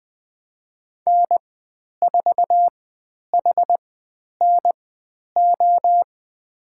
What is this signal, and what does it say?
Morse code, heard as a keyed tone, about 20 wpm, spaced as for 10 wpm, 700 Hz. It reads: N4HNO
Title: 109 Lesson 18g - Letter m - Call Signs 20wpm